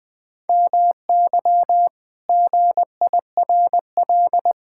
Morse code, heard as a keyed tone, about 21 wpm, 700 Hz. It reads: MY GIRL
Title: Pangram Challenge v2 - 20wpm